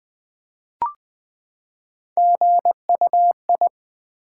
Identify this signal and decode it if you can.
Morse code, heard as a keyed tone, about 20 wpm, 700 Hz. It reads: GUI
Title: Build A Word - 3 Letters At A Time - 9-Letter Words - 20wpm